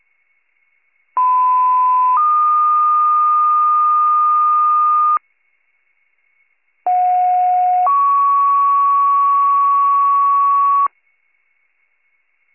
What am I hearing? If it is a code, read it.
Two-tone page, heard as a one-second tone, then a three-second tone, about 1010 then 1220 Hz, about 730 then 1120 Hz. It is heard from 1.2 to 10.9 s.